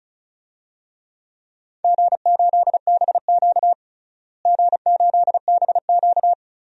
Morse code, heard as a keyed tone, about 34 wpm, 700 Hz. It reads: G8BQ G8BQ
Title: DXCC-100 Call Signs - 4 Characters - Repeated 3x - 35wpm